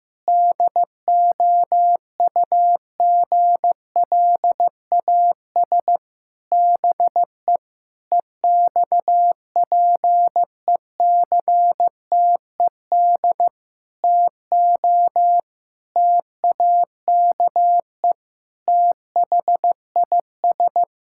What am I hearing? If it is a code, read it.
Morse code, heard as a keyed tone, about 16 wpm, 700 Hz. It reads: DOUGLAS BE EXPECTED TO TAKE THIS
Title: Pangram Challenge v2 - 15wpm